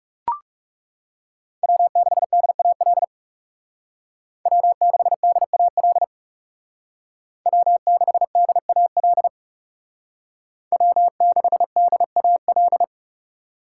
Morse code, heard as a keyed tone, about 35 wpm, 700 Hz. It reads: W6DAL W6DAL W6DAL W6DAL
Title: Speed Racing - USA 1x3 Call Signs 30wpm